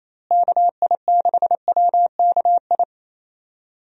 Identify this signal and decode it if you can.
Morse code, heard as a keyed tone, about 28 wpm, 700 Hz. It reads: KI6WKI
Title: USA Call Signs - 6 Characters - Repeated 3x - 28wpm